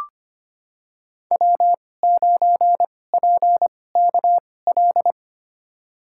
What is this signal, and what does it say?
Morse code, heard as a keyed tone, about 25 wpm, spaced as for 18 wpm, 700 Hz. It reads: W9PKL
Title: USA Call Signs - 5 Characters - 2x Character Spacing - 25wpm